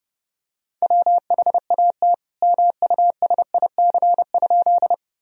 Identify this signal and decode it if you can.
Morse code, heard as a keyed tone, about 32 wpm, 700 Hz. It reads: WHAT MUSIC?
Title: Sentences from Top 1000 Words - Review Part 1 - 30wpm